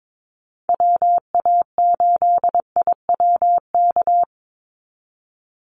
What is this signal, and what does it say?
Morse code, heard as a keyed tone, about 22 wpm, 700 Hz. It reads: WA8IWK
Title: Call Signs Repeated 2x - 22wpm